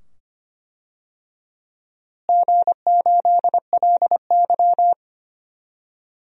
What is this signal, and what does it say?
Morse code, heard as a keyed tone, about 25 wpm, 700 Hz. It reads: G8LY
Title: DXCC-100 Call Signs - 4 Characters - Repeated 2x - 25wpm